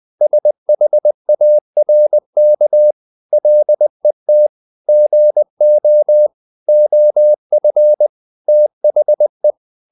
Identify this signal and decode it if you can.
Morse code, heard as a keyed tone, about 20 wpm, 600 Hz. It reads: SHARK LET GO OF THE
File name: the old man and the sea part 2 20wpm